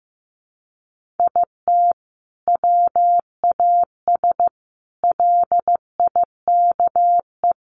Morse code, heard as a keyed tone, about 15 wpm, 700 Hz. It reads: IT WAS LIKE
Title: Sentences from Top 300 Words 15wpm